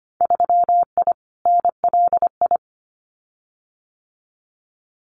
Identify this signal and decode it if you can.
Morse code, heard as a keyed tone, about 25 wpm, 690 Hz. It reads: 3I NLI
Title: Winter Field Day Exchanges Encore 25wpm